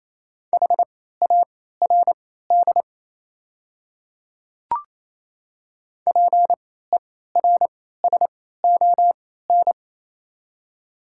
Morse code, heard as a keyed tone, about 28 wpm, spaced as for 16 wpm, 700 Hz. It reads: HARD PERSON
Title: 3x Character Spacing - Top 500 Words - 28wpm